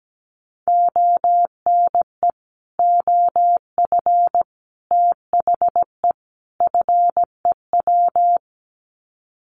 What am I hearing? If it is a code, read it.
Morse code, heard as a keyed tone, about 17 wpm, 700 Hz. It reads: ONE OF THE FEW
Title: Sets of 4 Words 17wpm